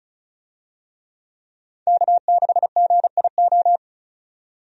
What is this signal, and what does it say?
Morse code, heard as a keyed tone, about 36 wpm, 700 Hz. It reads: K6GIO
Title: USA Call Signs - 5 Characters - 35wpm